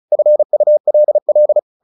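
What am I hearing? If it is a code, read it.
Morse code, heard as a keyed tone, about 34 wpm, 600 Hz. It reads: FULL
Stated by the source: Morse practice at 35 wpm from CW med Gnister, sending Ken Follett's novel Never in English.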